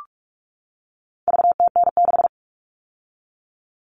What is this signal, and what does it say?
Morse code, heard as a keyed tone, about 44 wpm, 700 Hz. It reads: VTN6
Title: All Cut Numbers - 4-Digit 45wpm